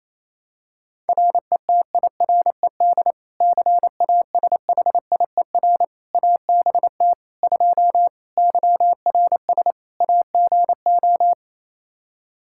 Morse code, heard as a keyed tone, about 27 wpm, 700 Hz. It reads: RETIRED CASHIER ABT 2 YRS AGO
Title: K1USN SST QSO Practice 28wpm